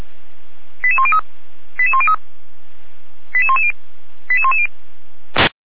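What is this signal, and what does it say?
Five-tone selective call (ZVEI), heard as quick runs of five tones, tones of about 70 ms. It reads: 88103 88103 88110 88110
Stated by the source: ZVEI five-tone selective call received by SDR, from Feuerwehr Orlamünde.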